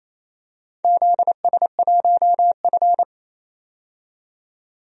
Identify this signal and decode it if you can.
Morse code, heard as a keyed tone, about 28 wpm, 700 Hz. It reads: ZS1F
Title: DXCC-100 Call Signs - 4 Characters - 28wpm